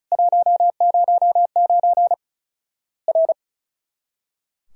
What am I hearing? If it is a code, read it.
Morse code, heard as a keyed tone, about 36 wpm, 700 Hz. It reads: 109 R
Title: Worked All Europe DX Contest - QTC Exchange - 35wpm